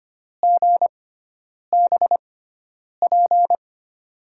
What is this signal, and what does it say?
Morse code, heard as a keyed tone, about 25 wpm, spaced as for 9 wpm, 700 Hz. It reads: GBP
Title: USA Call Signs - 6 Characters - 6x Character Spacing - 25wpm